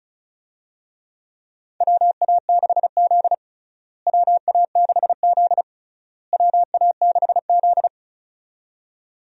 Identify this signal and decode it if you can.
Morse code, heard as a keyed tone, about 36 wpm, 700 Hz. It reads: WA6Z WA6Z WA6Z
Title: USA Call Signs - 4 Characters - Repeated 3x - 35wpm